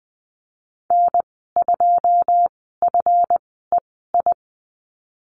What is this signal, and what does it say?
Morse code, heard as a keyed tone, about 20 wpm, spaced as for 15 wpm, 700 Hz. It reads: N2FEI
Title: 147 Lesson 24g - Letter y - Call Signs 20wpm